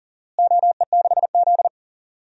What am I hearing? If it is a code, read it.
Morse code, heard as a keyed tone, about 40 wpm, 700 Hz. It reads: OE6Z
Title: DXCC-100 Call Signs - 4 Characters - Repeated 2x - 40wpm